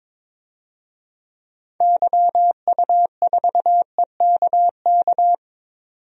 Morse code, heard as a keyed tone, about 22 wpm, 700 Hz. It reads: YU4EKK
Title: DXCC-100 Call Signs - 6 Characters - 22wpm